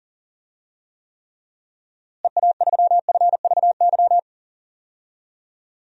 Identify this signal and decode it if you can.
Morse code, heard as a keyed tone, about 40 wpm, 700 Hz. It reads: EA3FVY
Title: DXCC-100 Call Signs - 6 Characters - Repeated 3x - 40wpm